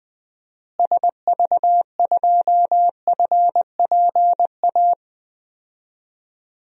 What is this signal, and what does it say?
Morse code, heard as a keyed tone, about 20 wpm, 700 Hz. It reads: SV2FPA
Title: DXCC-100 Call Signs - 6 Characters - Repeated 3x - 20wpm